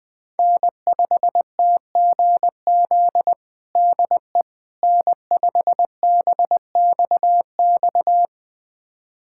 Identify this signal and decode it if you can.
Morse code, heard as a keyed tone, about 20 wpm, 700 Hz. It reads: N5TGZ DE N5BXX
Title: USA Callsign DE Callsign - 5 Characters - 20wpm